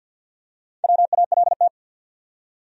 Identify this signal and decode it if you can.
Morse code, heard as a keyed tone, about 48 wpm, 700 Hz. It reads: WALT
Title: USA Common Names - Max 4 Letters - 50wpm